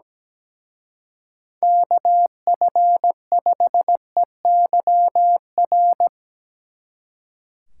Morse code, heard as a keyed tone, about 19 wpm, 700 Hz. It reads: KF5EYR
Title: Speed Racing - USA 2x3 Call Signs 17wpm